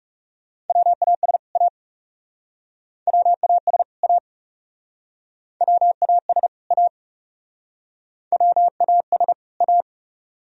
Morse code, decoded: WAS A WAS A WAS A WAS A

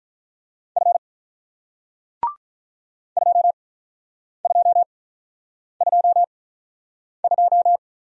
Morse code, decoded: F 2222

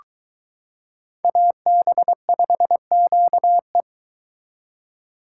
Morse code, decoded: AB5QE

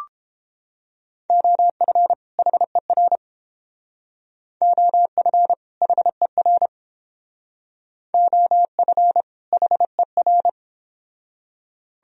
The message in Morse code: OF HER OF HER OF HER

